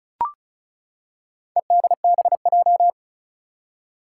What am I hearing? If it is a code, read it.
Morse code, heard as a keyed tone, about 36 wpm, 700 Hz. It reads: EDBJ
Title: Four Challenging Letter-Number Combinations 35wpm